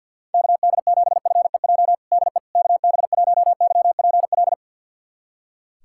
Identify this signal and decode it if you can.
Morse code, heard as a keyed tone, about 46 wpm, 700 Hz. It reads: KD7FEJ DE KB1YPL
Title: USA Callsign DE Callsign - 6 Characters - 50wpm